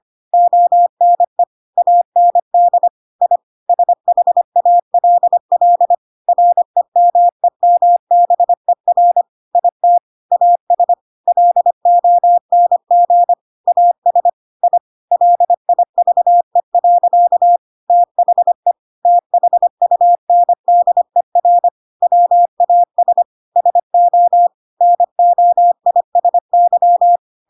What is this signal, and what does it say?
Morse code, heard as a keyed tone, about 25 wpm, 700 Hz. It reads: ONE AND I SHALL REMEMBER IT AS LONG AS I LIVE. THE THUNDER WAS SO NOISY
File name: Pinocchio / Pinocchio ch7 25 25 ik1hge